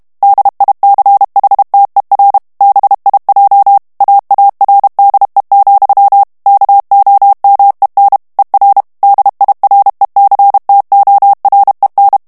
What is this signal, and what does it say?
Morse code, heard as a keyed tone, about 32 wpm, 800 Hz. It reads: DICHTER BIJ AARDE, KOMEN ER DIRECTOREN
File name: HOLLANDAIS texte clair 1 vitesse 30 mots minute